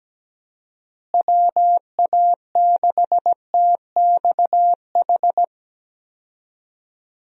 Morse code, heard as a keyed tone, about 17 wpm, 700 Hz. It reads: WA6TXH
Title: USA Call Signs - 6 Characters - Repeated 2x - 17wpm